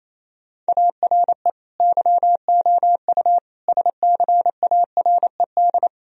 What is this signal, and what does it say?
Morse code, heard as a keyed tone, about 28 wpm, 700 Hz. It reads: ARE YOU SCARED
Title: Sentences from Top 1000 Words - Review Part 4 - 28wpm